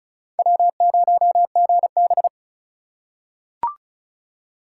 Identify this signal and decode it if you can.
Morse code, heard as a keyed tone, about 36 wpm, 700 Hz. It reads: W0GB
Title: USA Call Signs - 4 Characters - 4x Character Spacing - 35wpm